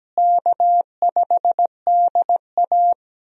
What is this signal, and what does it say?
Morse code, heard as a keyed tone, about 17 wpm, 700 Hz. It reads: K5DA
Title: USA Call Signs - 4 Characters - Repeated 2x - 17wpm